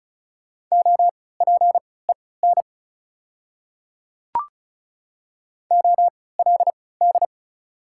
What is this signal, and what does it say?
Morse code, heard as a keyed tone, about 36 wpm, spaced as for 20 wpm, 690 Hz. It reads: OPEN OLD
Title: 3x Character Spacing - Top 500 Words - 35wpm